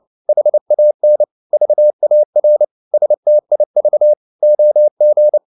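Morse code, heard as a keyed tone, about 29 wpm, 600 Hz. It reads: HAN VAR STIV OG
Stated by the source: Fast (QRQ) Morse at 29 wpm, from CW med Gnister.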